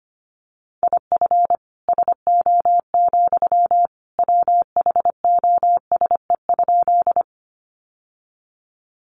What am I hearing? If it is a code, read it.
Morse code, heard as a keyed tone, about 25 wpm, 700 Hz. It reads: IF SO, WHOSE?
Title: Sentences from Top 1200 Words 25wpm